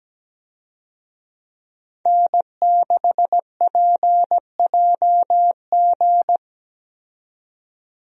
Morse code, heard as a keyed tone, about 17 wpm, 700 Hz. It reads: N6PJG
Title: USA Call Signs - 5 Characters - 17wpm